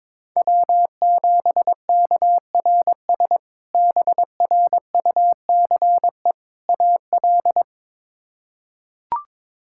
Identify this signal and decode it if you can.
Morse code, heard as a keyed tone, about 22 wpm, 700 Hz. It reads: W7KRS BRUCE AL
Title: K1USN SST QSO Practice - 22wpm